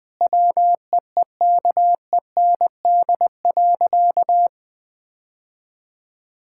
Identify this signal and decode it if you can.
Morse code, heard as a keyed tone, about 21 wpm, 700 Hz. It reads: WEEKEND.
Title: Difficult Sentences with Hints - 20wpm